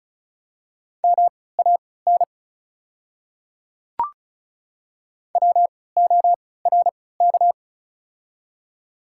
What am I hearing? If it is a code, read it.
Morse code, heard as a keyed tone, about 36 wpm, spaced as for 20 wpm, 690 Hz. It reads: MAN WORK
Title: 3x Character Spacing - Top 200 Words - 35wpm